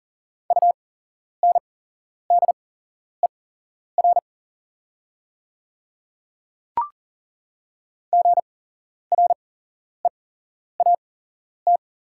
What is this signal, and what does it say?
Morse code, heard as a keyed tone, about 42 wpm, spaced as for 11 wpm, 690 Hz. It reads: UNDER GREAT
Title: 8x Character Spacing - Top 200 Words - 40wpm